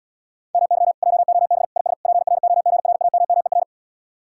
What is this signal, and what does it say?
Morse code, heard as a keyed tone, about 77 wpm, 700 Hz. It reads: MY JOB IS PROGRAMMER